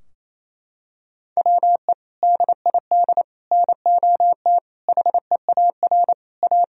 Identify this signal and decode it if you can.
Morse code, heard as a keyed tone, about 28 wpm, 700 Hz. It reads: WE DID NOT HEAR A
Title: Sentences from Top 1000 Words - Review Part 4 - 28wpm